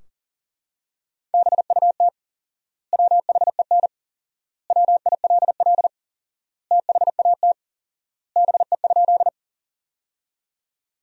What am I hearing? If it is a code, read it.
Morse code, heard as a keyed tone, about 40 wpm, 700 Hz. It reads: BUT WHEN WILL THAT BE?
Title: 4x Word-Spacing - Sentences from Top 100 Words - 40wpm